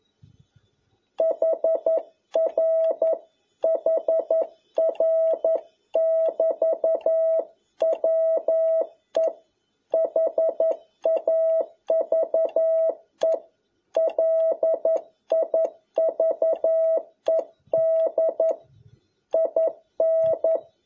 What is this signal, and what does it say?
Morse code, heard as a keyed tone, about 11 wpm, 650 Hz. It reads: HR HR <BT> WE HAVE LIVED IN
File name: jh8cbh / 2021cw